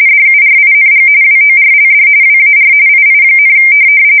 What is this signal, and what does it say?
RTTY, about 45.45 baud. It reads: DURING WORLD WAR II,
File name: fsk170 / USS-New-Jersey